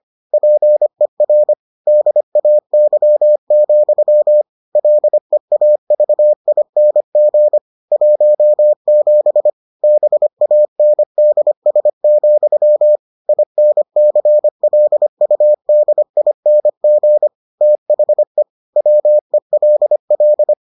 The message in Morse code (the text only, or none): PER DAY, LEAVING 17 BANDS, INCLUDING THE WELL